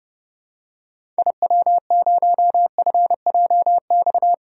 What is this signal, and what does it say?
Morse code, heard as a keyed tone, about 31 wpm, 700 Hz. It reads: IW0FJX
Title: DXCC-100 Call Signs - 6 Characters - 30wpm